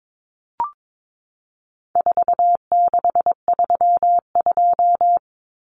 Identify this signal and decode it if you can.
Morse code, heard as a keyed tone, about 22 wpm, 700 Hz. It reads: 4632
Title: Common Cut Numbers - 4-Digit 22wpm